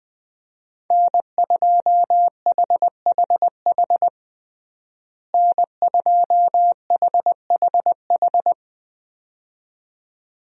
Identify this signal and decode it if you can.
Morse code, heard as a keyed tone, about 21 wpm, 700 Hz. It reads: N2HHH N2HHH
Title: USA Call Signs - 5 Characters - Repeated 2x - 20wpm